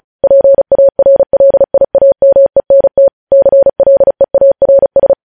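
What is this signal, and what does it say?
Morse code, heard as a keyed tone, about 36 wpm, 550 Hz. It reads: PARLIAMENT CLEARS
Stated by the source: News headlines sent in Morse at 35 WPM.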